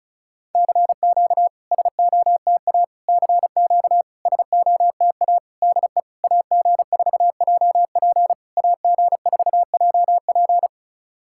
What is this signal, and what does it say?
Morse code, heard as a keyed tone, about 36 wpm, 690 Hz. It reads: CQ SOTA CQ SOTA DE AG4JP AG4JP
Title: SOTA QSO Practice 35wpm